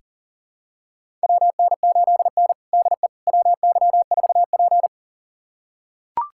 Morse code, heard as a keyed tone, about 40 wpm, 700 Hz. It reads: WN8N DE WY4P
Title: USA Callsign DE Callsign - 4 Characters - 40wpm